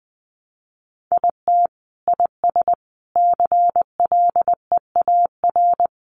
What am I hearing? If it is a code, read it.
Morse code, heard as a keyed tone, about 20 wpm, 700 Hz. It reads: IT IS CLEAR
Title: Sets of 5 Words Encore 20wpm